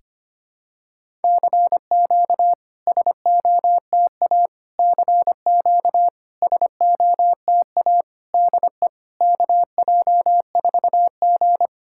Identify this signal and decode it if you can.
Morse code, heard as a keyed tone, about 25 wpm, 700 Hz. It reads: CQ SOTA CQ SOTA DE KJ4G
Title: SOTA QSO Practice 25wpm